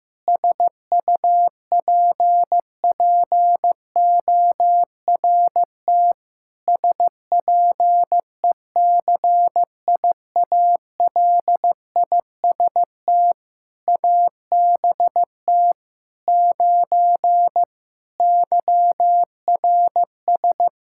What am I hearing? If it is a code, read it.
Morse code, heard as a keyed tone, about 15 wpm, 700 Hz. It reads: SUPPORT SPECIALIST ABT 9 YRS
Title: Practice QSO - Exchange 3 - More Info - 15wpm